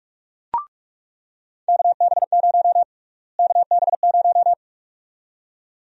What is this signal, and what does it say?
Morse code, heard as a keyed tone, about 46 wpm, 700 Hz. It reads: KB0 KB0
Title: USA Call Sign Prefixes - Repeated 2x - 45wpm